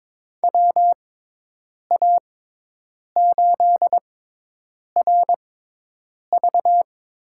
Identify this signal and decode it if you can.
Morse code, heard as a keyed tone, about 22 wpm, spaced as for 8 wpm, 700 Hz. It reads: WA8RV
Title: USA Call Signs - 6x Character Spacing - 22wpm